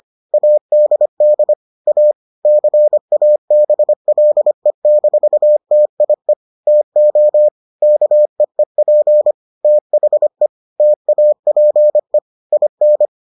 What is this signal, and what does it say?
Morse code, heard as a keyed tone, about 25 wpm, 600 Hz. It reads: ADD A CABLE-TIE TO KEEP THE TAPE IN